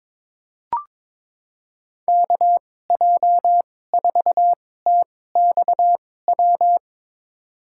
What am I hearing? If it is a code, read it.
Morse code, heard as a keyed tone, about 21 wpm, spaced as for 16 wpm, 700 Hz. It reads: KJ4TXW
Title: USA Call Signs - 6 Characters - 2x Character Spacing - 22wpm